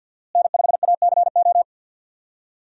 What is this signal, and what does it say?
Morse code, heard as a keyed tone, about 52 wpm, 700 Hz. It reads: N5ACO
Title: USA Call Signs - 5 Characters - Repeated 3x - 50wpm